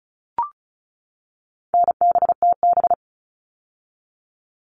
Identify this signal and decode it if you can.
Morse code, heard as a keyed tone, about 34 wpm, 700 Hz. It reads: NBTB
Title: All Cut Numbers - 4-Digit 35wpm